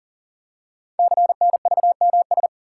Morse code, heard as a keyed tone, about 38 wpm, 690 Hz. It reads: CNVMS
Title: Code Groups - 5 Letters - 40wpm